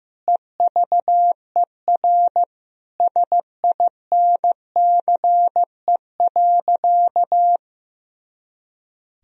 Morse code, heard as a keyed tone, about 16 wpm, 700 Hz. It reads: EVER SINCE.
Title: Sentences from Top 400 Words and Abbreviations 15wpm